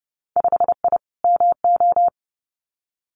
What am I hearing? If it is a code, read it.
Morse code, heard as a keyed tone, about 30 wpm, 700 Hz. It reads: 5I MO
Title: Winter Field Day Exchanges Encore 30wpm